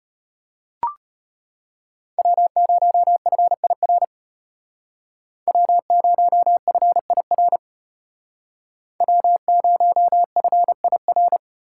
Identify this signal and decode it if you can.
Morse code, heard as a keyed tone, about 30 wpm, 700 Hz. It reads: W0FIR W0FIR W0FIR
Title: Speed Racing - USA 1x3 Call Signs 25wpm